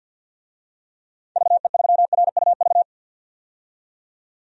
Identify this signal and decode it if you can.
Morse code, heard as a keyed tone, about 52 wpm, 700 Hz. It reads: VE3RUV
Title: DXCC-100 Call Signs - 6 Characters - 50wpm